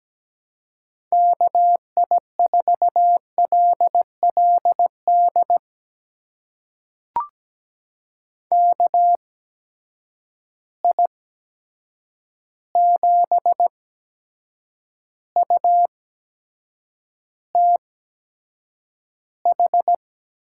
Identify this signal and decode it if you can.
Morse code, heard as a keyed tone, about 17 wpm, spaced as for 7 wpm, 700 Hz. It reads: KI4LLD KI7UTH
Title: USA Call Signs - 6 Characters - 8x Character Spacing - 17wpm